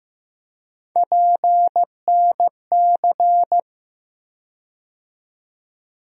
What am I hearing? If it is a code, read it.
Morse code, heard as a keyed tone, about 15 wpm, 700 Hz. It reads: PNC
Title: USA Call Sign Postfixes - 3 Letters 15wpm